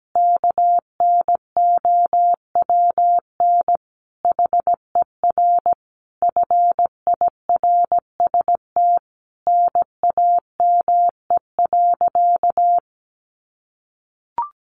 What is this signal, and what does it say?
Morse code, heard as a keyed tone, about 17 wpm, 700 Hz. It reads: KNOWN HER FIRST NAME.
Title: Sentences from Top 1200 Words 17wpm